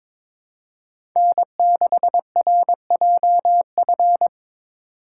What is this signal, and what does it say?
Morse code, heard as a keyed tone, about 22 wpm, 700 Hz. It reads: N6RJF
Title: USA Call Signs - 5 Characters - 4x Character Spacing - 22wpm